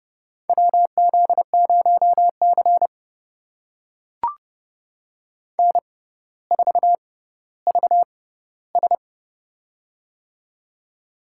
Morse code, heard as a keyed tone, about 30 wpm, spaced as for 15 wpm, 700 Hz. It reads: WZ0C N4VS
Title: USA Call Signs - 4 Characters - 6x Character Spacing - 30wpm